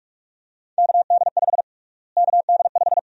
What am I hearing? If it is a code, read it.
Morse code, heard as a keyed tone, about 46 wpm, 700 Hz. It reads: KD5 KD5
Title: USA Call Sign Prefixes - Repeated 2x - 45wpm